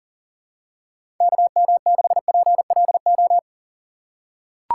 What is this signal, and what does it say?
Morse code, heard as a keyed tone, about 38 wpm, 700 Hz. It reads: KM6PLO
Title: USA Call Signs - 4x Character Spacing - 40wpm